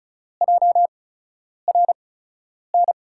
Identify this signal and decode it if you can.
Morse code, heard as a keyed tone, about 34 wpm, spaced as for 10 wpm, 700 Hz. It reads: JRN
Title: USA Call Signs - 5 Characters - 8x Character Spacing - 35wpm